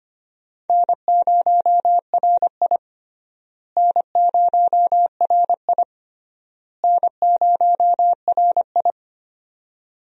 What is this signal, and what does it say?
Morse code, heard as a keyed tone, about 25 wpm, 700 Hz. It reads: N0RI N0RI N0RI
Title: USA Call Signs - 4 Characters - Repeated 3x - 25wpm